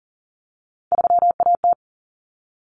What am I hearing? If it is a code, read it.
Morse code, heard as a keyed tone, about 40 wpm, 700 Hz. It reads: 3AT
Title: Common Cut Numbers - 3-Digit 40wpm